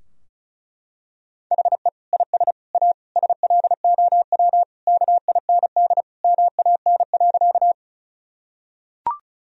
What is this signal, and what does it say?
Morse code, heard as a keyed tone, about 36 wpm, 700 Hz. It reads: HE IS A SLOW KIND MAN.
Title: Sentences from Top 1000 Words - Review Part 2 - 35wpm